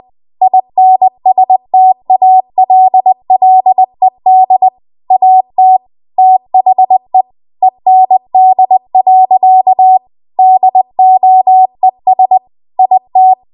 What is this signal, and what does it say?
Morse code, heard as a keyed tone, about 20 wpm, 750 Hz. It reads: INSTALLED AT THE END. DOES IT
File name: Morse / 20WPM